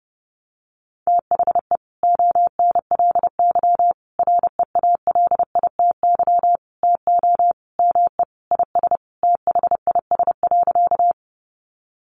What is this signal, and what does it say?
Morse code, heard as a keyed tone, about 30 wpm, 700 Hz. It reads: THE ONLY REALITY TO ME IS THIS.
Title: Sentences from Top 1200 Words 30wpm